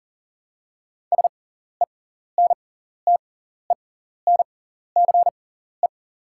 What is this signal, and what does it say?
Morse code, heard as a keyed tone, about 39 wpm, spaced as for 14 wpm, 700 Hz. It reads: SENTENCE
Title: 6x Character Spacing - Top 500 Words - 40wpm